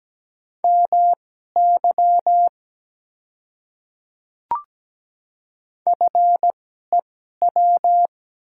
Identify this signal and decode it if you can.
Morse code, heard as a keyed tone, about 17 wpm, spaced as for 12 wpm, 700 Hz. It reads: MY FEW
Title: 2x Character Spacing - Top 300 Words - 17wpm